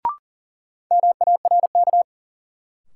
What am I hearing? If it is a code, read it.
Morse code, heard as a keyed tone, about 38 wpm, 700 Hz. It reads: MARK
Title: Sentences from Top 300 Words - New Word Drill - 40wpm